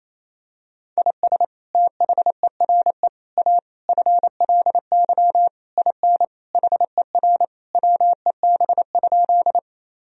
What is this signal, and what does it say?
Morse code, heard as a keyed tone, about 28 wpm, 700 Hz. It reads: IS THERE A FLY IN HER WEB?
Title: Sentences from Top 1000 Words - Review Part 4 - 28wpm